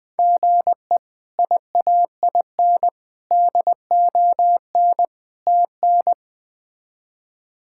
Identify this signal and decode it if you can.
Morse code, heard as a keyed tone, about 20 wpm, 700 Hz. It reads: GE IAIN DON TN
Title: K1USN SST QSO Practice - 20wpm